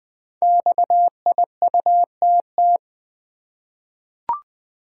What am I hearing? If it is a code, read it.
Morse code, heard as a keyed tone, about 20 wpm, 700 Hz. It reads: XIUTT
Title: Code Groups - 5 Letters - 20wpm